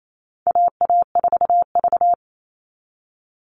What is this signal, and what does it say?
Morse code, heard as a keyed tone, about 28 wpm, 700 Hz. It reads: AA4V
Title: All Cut Numbers - 4-Digit 28wpm